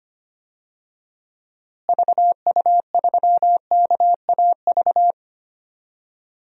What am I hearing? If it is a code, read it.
Morse code, heard as a keyed tone, about 25 wpm, 700 Hz. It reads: VU3KAV
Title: DXCC-100 Call Signs - 6 Characters - Repeated 3x - 25wpm